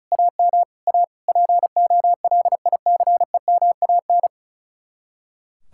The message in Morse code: AM A POLICEMAN

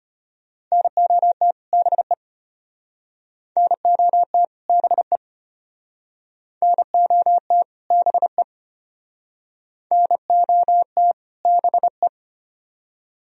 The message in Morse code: NOT BE NOT BE NOT BE NOT BE

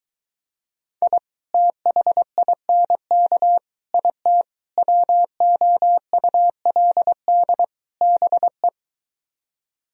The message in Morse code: I THINK IT WOULD BE

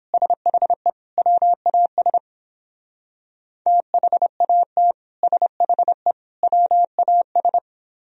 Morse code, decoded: SHE WAS THAT SHE WAS

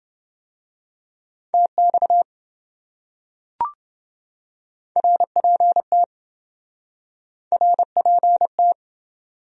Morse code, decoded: TX RPT RPT

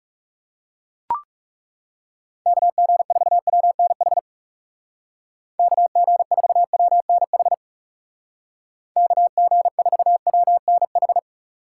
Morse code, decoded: KG4WNH KG4WNH KG4WNH

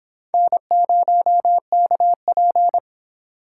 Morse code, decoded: N0KP

